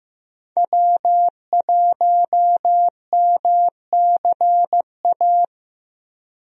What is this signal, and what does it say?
Morse code, heard as a keyed tone, about 15 wpm, 700 Hz. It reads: W1MCA
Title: USA Call Signs - 5 Characters - 15wpm